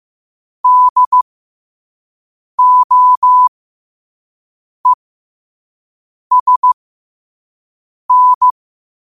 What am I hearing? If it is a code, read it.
Morse code, heard as a keyed tone, about 15 wpm, spaced as for 5 wpm, 1000 Hz. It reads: DOESN
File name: qotdfiles / 5wpm